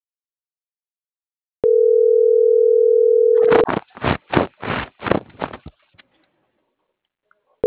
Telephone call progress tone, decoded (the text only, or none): ringback tone